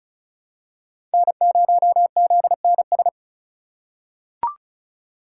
Morse code, heard as a keyed tone, about 36 wpm, 700 Hz. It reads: N0ZNS
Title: USA Call Signs - 4x Character Spacing - 35wpm